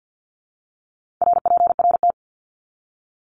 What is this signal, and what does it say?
Morse code, heard as a keyed tone, about 50 wpm, 700 Hz. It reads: RPRT
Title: Abbreviations 50wpm